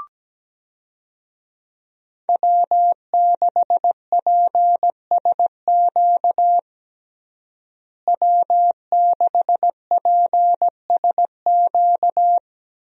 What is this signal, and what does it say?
Morse code, heard as a keyed tone, about 17 wpm, 700 Hz. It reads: W6PSQ W6PSQ
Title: USA Call Signs - 5 Characters - Repeated 3x - 17wpm